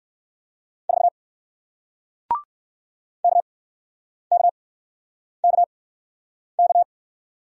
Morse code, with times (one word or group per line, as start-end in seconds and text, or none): 0.9-1.1 4
3.2-6.8 KKKK